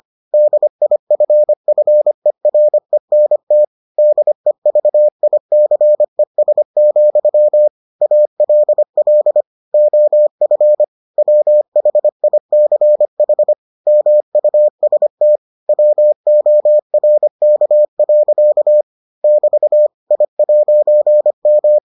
Morse code, heard as a keyed tone, about 25 wpm, 600 Hz. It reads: DIFFERENT DEVICES, ALL OF WHICH MUST WORK. <BT> I'M